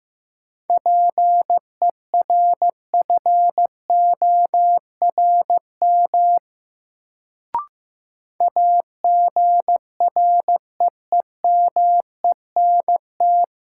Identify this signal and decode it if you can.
Morse code, heard as a keyed tone, about 15 wpm, 700 Hz. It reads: PERFORM AGREEMENT
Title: Sentences from Top 1500 Words - New Word Drill Alt - 15wpm